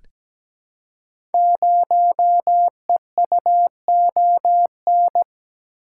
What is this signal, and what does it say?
Morse code, heard as a keyed tone, about 17 wpm, 700 Hz. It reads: 0EUON
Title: Code Groups - 5 Letter:Numbers - 17wpm